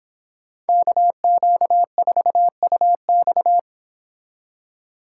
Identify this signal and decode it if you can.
Morse code, heard as a keyed tone, about 26 wpm, 700 Hz. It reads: KQ4UX